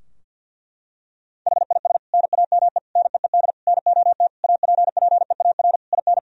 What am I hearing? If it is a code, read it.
Morse code, heard as a keyed tone, about 50 wpm, 700 Hz. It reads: HIS NAME NEED NOT APPEAR IN
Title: Sentences from Top 1000 Words - Review Part 2 - 50wpm